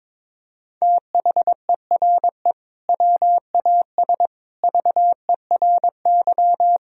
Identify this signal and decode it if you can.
Morse code, heard as a keyed tone, about 22 wpm, 700 Hz. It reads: THERE WAS VERY
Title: Sentences from Top 1000 Words - Review Part 1 - 22wpm